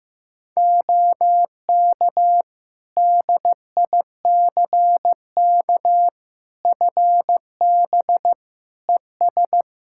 Morse code, heard as a keyed tone, about 16 wpm, 700 Hz. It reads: OK DICK FB ES
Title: Practice QSO - Exchange 2 - Info - 15wpm